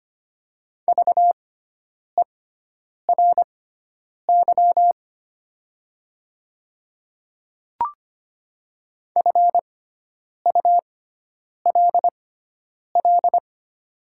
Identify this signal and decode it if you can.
Morse code, heard as a keyed tone, about 25 wpm, spaced as for 9 wpm, 700 Hz. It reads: VERY FULL
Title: 6x Character Spacing - Top 500 Words - 25wpm